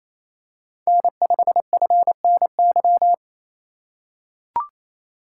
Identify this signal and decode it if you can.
Morse code, heard as a keyed tone, about 28 wpm, 700 Hz. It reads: N5FNY
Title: USA Call Signs - 5 Characters - 4x Character Spacing - 28wpm